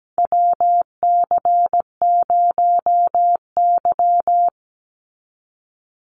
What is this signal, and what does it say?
Morse code, heard as a keyed tone, about 17 wpm, 700 Hz. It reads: WC0Y
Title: NA SOTA Chaser Callsigns 17wpm